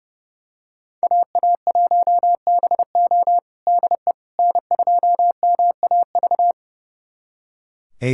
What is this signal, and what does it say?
Morse code, heard as a keyed tone, about 32 wpm, 700 Hz. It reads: AA1BO DE N2MAV
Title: USA Callsign DE Callsign - 5 Characters - 30wpm